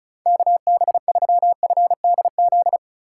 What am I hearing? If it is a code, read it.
Morse code, heard as a keyed tone, about 36 wpm, 700 Hz. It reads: KB3FDZ
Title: USA Call Signs - 6 Characters - 4x Character Spacing - 35wpm